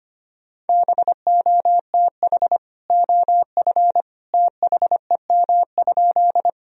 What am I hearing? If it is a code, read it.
Morse code, heard as a keyed tone, about 25 wpm, 700 Hz. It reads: BOTH OF THEM?
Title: Sentences from Top 1000 Words - Review Part 1 - 25wpm